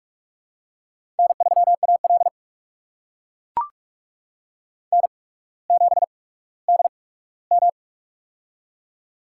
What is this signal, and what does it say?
Morse code, heard as a keyed tone, about 44 wpm, spaced as for 19 wpm, 700 Hz. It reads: N3AL N7DM
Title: USA Call Signs - 4 Characters - 8x Character Spacing - 45wpm